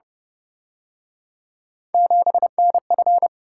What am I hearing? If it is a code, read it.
Morse code, heard as a keyed tone, about 29 wpm, 700 Hz. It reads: 7NF
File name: Speed-Racing - Three Letter-Number Combinations - 25wpm